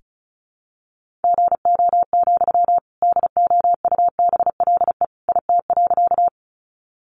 Morse code, heard as a keyed tone, about 34 wpm, 700 Hz. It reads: GO, DOUBLE IT.
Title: Sentences from Top 1400 Words 35wpm